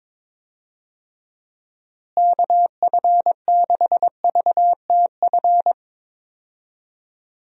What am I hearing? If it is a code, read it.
Morse code, heard as a keyed tone, about 22 wpm, 700 Hz. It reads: KF6VTF
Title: USA Call Signs - 6 Characters - Repeated 3x - 22wpm